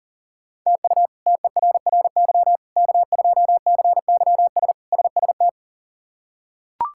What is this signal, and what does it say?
Morse code, heard as a keyed tone, about 40 wpm, 700 Hz. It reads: TU TERRY K2CYS SST
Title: K1USN SST QSO Practice - 40wpm